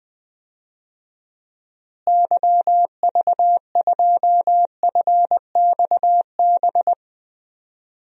Morse code, heard as a keyed tone, about 20 wpm, 700 Hz. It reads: YV2FXB
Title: DXCC-100 Call Signs - 6 Characters - Repeated 3x - 20wpm